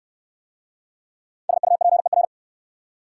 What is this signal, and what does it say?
Morse code, heard as a keyed tone, about 60 wpm, 700 Hz. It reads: SUPER